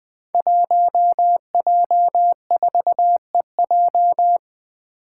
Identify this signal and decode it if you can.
Morse code, heard as a keyed tone, about 21 wpm, 700 Hz. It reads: 1J4EJ
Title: Five Challenging Letter-Number Combinations 20wpm